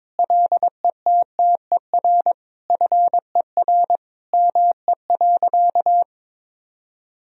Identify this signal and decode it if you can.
Morse code, heard as a keyed tone, about 22 wpm, 700 Hz. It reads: LETTER FER ME.
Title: Sentences from Top 300 Words and Abbreviations 22wpm